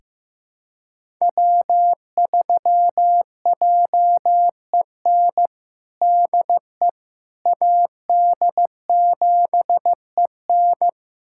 Morse code, heard as a keyed tone, about 16 wpm, 700 Hz. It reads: W3JEN DE AD7EN
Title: USA Callsign DE Callsign - 5 Characters - 15wpm